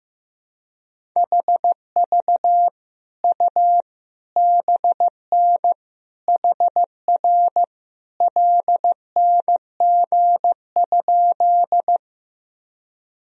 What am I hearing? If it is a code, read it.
Morse code, heard as a keyed tone, about 15 wpm, 700 Hz. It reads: HV U BN HR LNG?
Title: Sentences from Top 200 Words and Abbreviations 15wpm